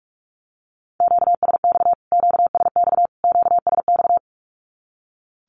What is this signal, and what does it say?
Morse code, heard as a keyed tone, about 44 wpm, 700 Hz. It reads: QSX QSX QSX
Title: Q-Codes Repeated 3x 45wpm